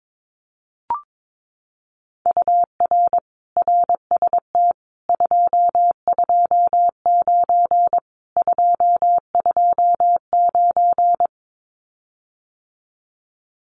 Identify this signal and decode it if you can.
Morse code, heard as a keyed tone, about 22 wpm, 700 Hz. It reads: UR RST 229 229
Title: Common QSO Phrases Encore 22wpm